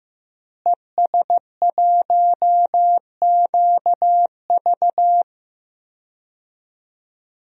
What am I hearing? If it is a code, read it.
Morse code, heard as a keyed tone, about 15 wpm, 700 Hz. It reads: ES1QV
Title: DXCC-100 Call Signs - 5 Characters - Repeated 3x - 15wpm